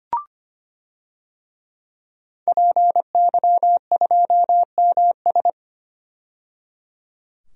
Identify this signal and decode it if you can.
Morse code, heard as a keyed tone, about 25 wpm, 700 Hz. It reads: PY2MS
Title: DXCC-100 Call Signs - 5 Characters - 25wpm